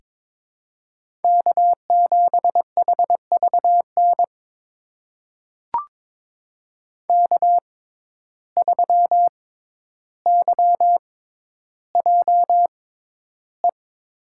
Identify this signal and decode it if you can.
Morse code, heard as a keyed tone, about 22 wpm, spaced as for 11 wpm, 700 Hz. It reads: K7HVN K3YJE
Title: USA Call Signs - 5 Characters - 6x Character Spacing - 22wpm